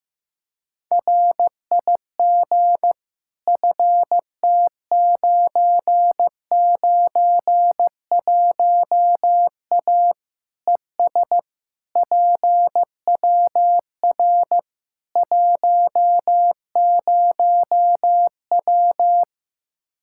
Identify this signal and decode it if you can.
Morse code, heard as a keyed tone, about 15 wpm, 700 Hz. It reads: RIG FT991A ES PWR 10W
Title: K1USN SST QSO Practice 15wpm